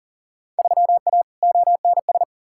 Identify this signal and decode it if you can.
Morse code, heard as a keyed tone, about 40 wpm, 700 Hz. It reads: 3A ONS
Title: Field Day Exchanges - 40wpm